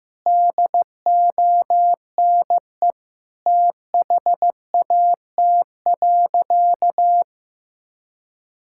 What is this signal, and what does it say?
Morse code, heard as a keyed tone, about 14 wpm, 700 Hz. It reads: DONE THAT.
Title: Sentences from Top 1600 Words 15wpm